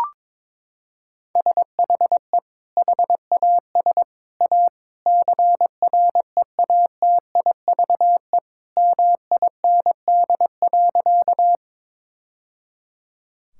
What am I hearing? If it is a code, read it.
Morse code, heard as a keyed tone, about 21 wpm, 700 Hz. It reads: SHE HAS A CREATIVE MIND.
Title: Sentences from Top 1000 Words - Review Part 5 - 22wpm